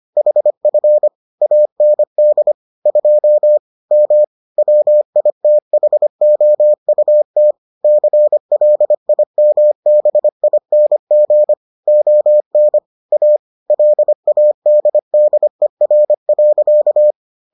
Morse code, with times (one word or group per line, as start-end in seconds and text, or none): 0.2-1.1 HF
1.4-2.5 AND
2.9-3.6 2
3.9-4.2 M
4.6-7.5 WITHOUT
7.8-11.5 CLIMBING
11.9-12.8 ON
13.1-13.4 A
13.7-17.1 LADDER.